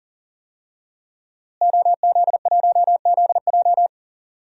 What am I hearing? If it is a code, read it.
Morse code, heard as a keyed tone, about 40 wpm, 700 Hz. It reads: OZ1ZJ
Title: DXCC-100 Call Signs - 5 Characters - Repeated 3x - 40wpm